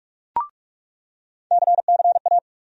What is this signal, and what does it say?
Morse code, heard as a keyed tone, about 46 wpm, 700 Hz. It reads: CCA
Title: USA Call Sign Postfixes - 3 Letters 45wpm